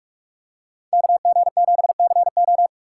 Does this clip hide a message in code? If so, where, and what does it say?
Yes: Morse code from 0.9 to 2.7 s, heard as a keyed tone, about 44 wpm, 700 Hz. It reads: KG7CO